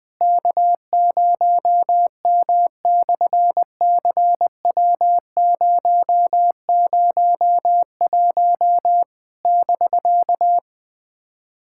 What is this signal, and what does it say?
Morse code, heard as a keyed tone, about 20 wpm, 700 Hz. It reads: K0M/CW001 <BK>
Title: SOTA QSO Practice 20wpm